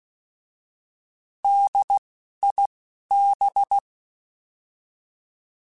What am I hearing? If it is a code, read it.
Morse code, heard as a keyed tone, about 16 wpm, 800 Hz. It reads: D I B
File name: Alphabetical / Alphabetical mp3s / Ds @10